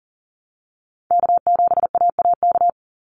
Morse code, heard as a keyed tone, about 40 wpm, 700 Hz. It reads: K7AAK
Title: USA Call Signs 40wpm